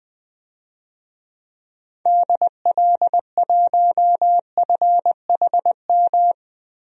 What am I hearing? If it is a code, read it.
Morse code, heard as a keyed tone, about 20 wpm, 700 Hz. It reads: DL1FHM
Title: DXCC-100 Call Signs - 6 Characters - 20wpm